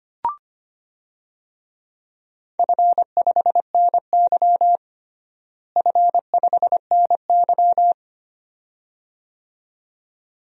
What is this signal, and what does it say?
Morse code, heard as a keyed tone, about 25 wpm, 700 Hz. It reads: F5NY F5NY
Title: DXCC-100 Call Signs - 4 Characters - Repeated 2x - 25wpm